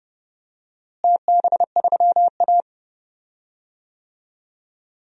Morse code, heard as a keyed tone, about 31 wpm, 700 Hz. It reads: TB3A
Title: DXCC-100 Call Signs - 4 Characters - Repeated 2x - 30wpm